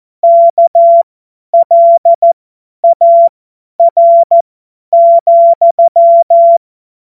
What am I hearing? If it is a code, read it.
Morse code, heard as a keyed tone, about 14 wpm, spaced as for 10 wpm, 680 Hz. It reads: KLAR,